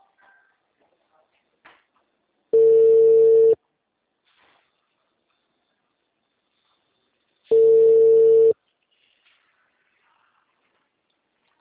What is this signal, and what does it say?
Ringback tone (China, 450 Hz).
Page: TLML PRUDENTE 1